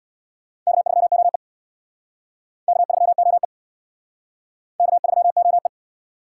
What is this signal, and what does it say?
Morse code, heard as a keyed tone, about 61 wpm, 700 Hz. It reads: K3OE K3OE K3OE
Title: Speed Racing - USA Call Signs 50wpm